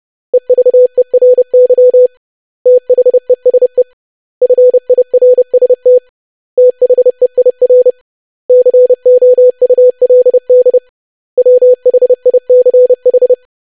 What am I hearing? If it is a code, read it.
Morse code, heard as a keyed tone, about 30 wpm, 500 Hz. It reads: EVERY THESE FIRST THEIR COULD WHICH
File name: lesson / web / basic100-25wpm